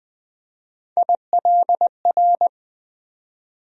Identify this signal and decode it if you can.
Morse code, heard as a keyed tone, about 20 wpm, 700 Hz. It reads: ILR
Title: USA Call Sign Postfixes - 3 Letters 20wpm